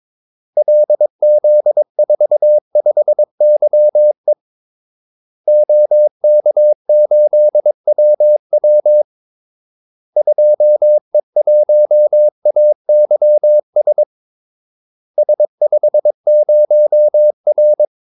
Morse code, heard as a keyed tone, about 22 wpm, 600 Hz. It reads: LZ45YE OK8WW 2E1AYS S50R
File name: Callsigns-22wpm